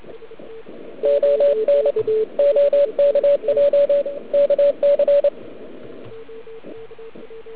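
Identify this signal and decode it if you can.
Morse code, heard as a keyed tone, about 27 wpm, 570 Hz. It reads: ON OKJ KC
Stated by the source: Morse from the OK QRP 2013 contest, run at 1 watt on a 12.3V lithium battery, Liberci, with OK1MKX also at 1W.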